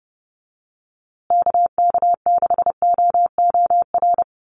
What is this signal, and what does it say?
Morse code, heard as a keyed tone, about 30 wpm, 700 Hz. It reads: KK6OOR
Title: USA Call Signs Repeated 3x 30wpm